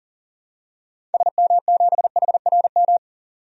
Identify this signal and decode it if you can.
Morse code, heard as a keyed tone, about 37 wpm, 700 Hz. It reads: SM7HRM
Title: DXCC-100 Call Signs - 6 Characters - 40wpm